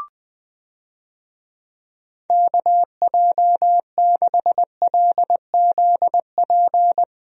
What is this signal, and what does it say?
Morse code, heard as a keyed tone, about 20 wpm, 700 Hz. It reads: KJ6LZP
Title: USA Call Signs - 6 Characters - Repeated 2x - 20wpm